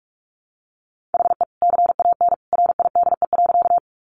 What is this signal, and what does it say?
Morse code, heard as a keyed tone, about 44 wpm, 700 Hz. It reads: HE CAN RIDE.
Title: Sentences from Top 1200 Words 45wpm